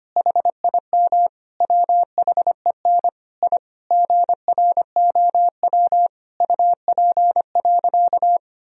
Morse code, heard as a keyed tone, about 25 wpm, 700 Hz. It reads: HIM WHEN I GROW UP.
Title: Sentences from Top 1000 Words - Review Part 1 - 25wpm